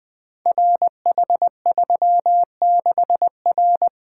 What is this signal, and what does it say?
Morse code, heard as a keyed tone, about 20 wpm, 700 Hz. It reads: RH36R
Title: Code Groups - 5 Letter:Numbers - 20wpm